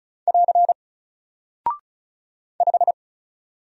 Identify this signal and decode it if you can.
Morse code, heard as a keyed tone, about 36 wpm, 700 Hz. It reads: <AR> 5
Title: Single Character (Letter:Number:Punctuation:Prosign) Alt 35wpm